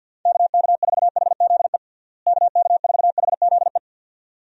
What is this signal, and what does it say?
Morse code, heard as a keyed tone, about 45 wpm, 700 Hz. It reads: KK4HZE KK4HZE
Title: USA Call Signs - 6 Characters - Repeated 2x - 50wpm